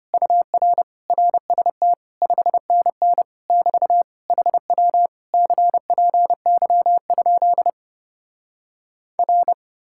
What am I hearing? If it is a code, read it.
Morse code, heard as a keyed tone, about 33 wpm, 700 Hz. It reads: UR RST 5NN <BT> HW CPY? R